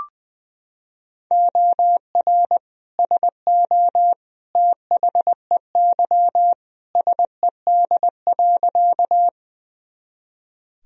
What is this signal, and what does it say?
Morse code, heard as a keyed tone, about 21 wpm, 700 Hz. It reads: OR SO THEY SED.
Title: Sentences from Top 100 Words and Abbreviations 20wpm